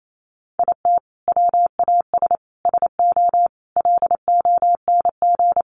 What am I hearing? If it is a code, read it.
Morse code, heard as a keyed tone, about 28 wpm, 700 Hz. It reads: IT WAS SO LONG
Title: Sentences from Top 500 Words 28wpm